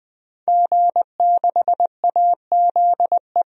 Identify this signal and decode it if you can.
Morse code, heard as a keyed tone, about 21 wpm, 700 Hz. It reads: G6AZE
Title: DXCC-100 Call Signs - 5 Characters - Repeated 2x - 20wpm